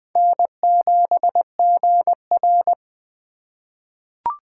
Morse code, heard as a keyed tone, about 20 wpm, 700 Hz. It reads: N7GR
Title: USA Call Signs - 4 Characters - 3x Character Spacing - 20wpm